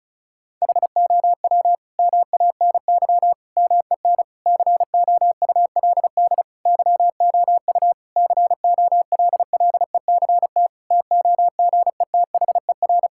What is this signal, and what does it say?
Morse code, heard as a keyed tone, about 34 wpm, 700 Hz. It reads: HOW MANY MEN COULD YOU COLLECT TOGETHER
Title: Sentences from Top 1600 Words 35wpm